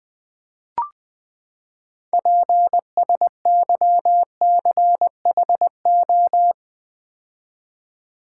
Morse code, heard as a keyed tone, about 20 wpm, 700 Hz. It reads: PSYCHO
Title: Build A Word - 6 Letters At A Time - 12-Letter Words - 20wpm